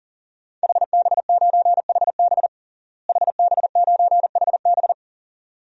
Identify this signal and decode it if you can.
Morse code, heard as a keyed tone, about 40 wpm, 700 Hz. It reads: HB9HB HB9HB
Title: DXCC-100 Call Signs - 5 Characters - Repeated 2x - 40wpm